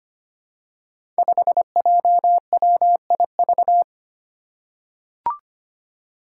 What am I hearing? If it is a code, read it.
Morse code, heard as a keyed tone, about 25 wpm, 700 Hz. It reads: 5JWIV
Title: Five Challenging Letter-Number Combinations 25wpm